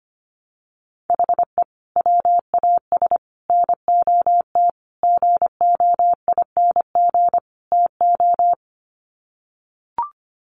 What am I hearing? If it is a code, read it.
Morse code, heard as a keyed tone, about 25 wpm, 700 Hz. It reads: HE WAS NOT GOING TO
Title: Sets of 5 Words 25wpm